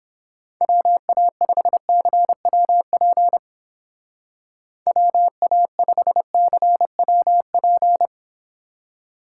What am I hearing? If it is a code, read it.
Morse code, heard as a keyed tone, about 30 wpm, 700 Hz. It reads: WA5CWP WA5CWP